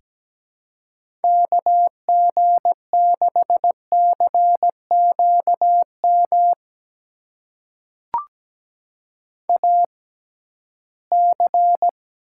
Morse code, heard as a keyed tone, about 17 wpm, spaced as for 13 wpm, 700 Hz. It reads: KG6CQM AC
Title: USA Call Signs - 6x Character Spacing - 17wpm